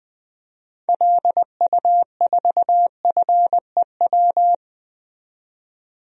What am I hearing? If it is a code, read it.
Morse code, heard as a keyed tone, about 21 wpm, 700 Hz. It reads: LU4FEW
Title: DXCC-100 Call Signs - 6 Characters - 20wpm